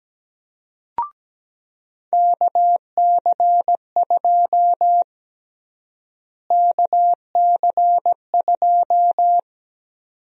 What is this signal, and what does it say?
Morse code, heard as a keyed tone, about 17 wpm, 700 Hz. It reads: KC2 KC2
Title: USA Call Sign Prefixes - Repeated 2x - 17wpm